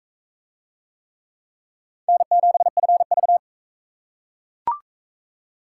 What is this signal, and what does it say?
Morse code, heard as a keyed tone, about 38 wpm, 700 Hz. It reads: N7FV